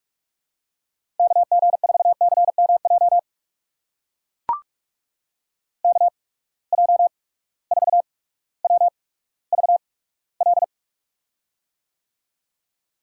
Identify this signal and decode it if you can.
Morse code, heard as a keyed tone, about 44 wpm, spaced as for 19 wpm, 700 Hz. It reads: KG4CMJ KJ4WVL
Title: USA Call Signs - 6 Characters - 8x Character Spacing - 45wpm